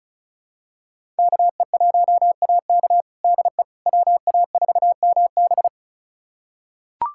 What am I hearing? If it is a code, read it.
Morse code, heard as a keyed tone, about 36 wpm, 700 Hz. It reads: KE1AK DE WA4MB
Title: USA Callsign DE Callsign - 5 Characters - 35wpm